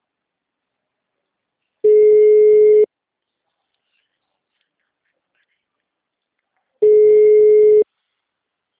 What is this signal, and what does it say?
Ringback tone (Europe, 425 Hz).
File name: out- -2001- - - .13123